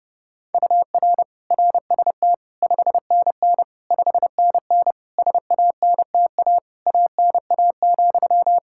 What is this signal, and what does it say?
Morse code, heard as a keyed tone, about 30 wpm, 700 Hz. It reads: UR RST 5NN 5NN SANTA ANA,
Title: Practice QSO - Exchange 1 - Beginning - 30wpm